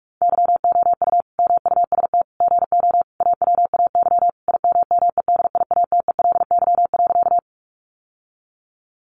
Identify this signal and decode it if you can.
Morse code, heard as a keyed tone, about 44 wpm, 700 Hz. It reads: YOU MUST GO AWAY IMMEDIATELY.
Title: Sentences from Top 1100 Words 45wpm